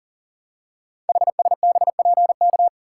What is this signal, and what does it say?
Morse code, heard as a keyed tone, about 43 wpm, 700 Hz. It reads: HSBPK
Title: Code Groups - 5 Letters - 40wpm